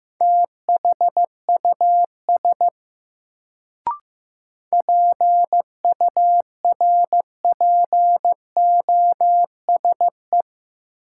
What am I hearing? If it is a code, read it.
Morse code, heard as a keyed tone, about 15 wpm, 700 Hz. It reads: THUS PURPOSE
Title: Sentences from Top 800 Words - New Word Drill - 15wpm